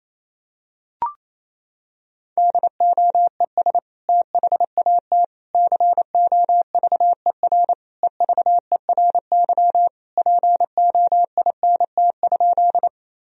Morse code, heard as a keyed tone, about 28 wpm, 700 Hz. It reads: DOES THAT COVER EVERY POINT?
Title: Sentences from Top 1000 Words - Review Part 1 - 28wpm